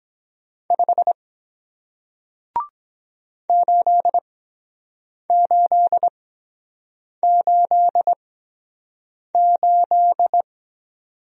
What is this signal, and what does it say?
Morse code, heard as a keyed tone, about 23 wpm, spaced as for 7 wpm, 700 Hz. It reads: 5 8888